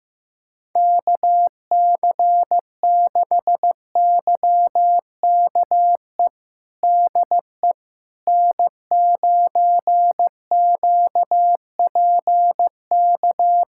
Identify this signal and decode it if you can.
Morse code, heard as a keyed tone, about 15 wpm, 700 Hz. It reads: KC6YKE DE N9QPK
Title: USA Callsign DE Callsign 15wpm